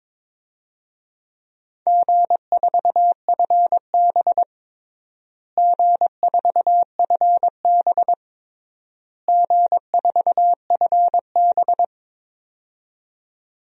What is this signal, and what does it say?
Morse code, heard as a keyed tone, about 22 wpm, 700 Hz. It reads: G4FB G4FB G4FB
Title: DXCC-100 Call Signs - 4 Characters - Repeated 3x - 22wpm